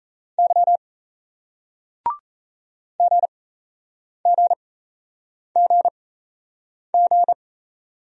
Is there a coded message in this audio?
Yes: Morse code from 0.4 to 7.3 s, heard as a keyed tone, about 38 wpm, spaced as for 8 wpm, 700 Hz. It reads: Y GGGG